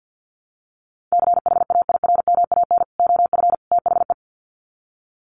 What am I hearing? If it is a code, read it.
Morse code, heard as a keyed tone, about 50 wpm, 700 Hz. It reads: CHAIRMAN OF THE
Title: Sets of 3 Words 50wpm